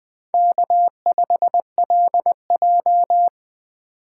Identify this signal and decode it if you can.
Morse code, heard as a keyed tone, about 24 wpm, 700 Hz. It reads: K5LJ
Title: Speed Racing - USA 1x2 Call Signs 20wpm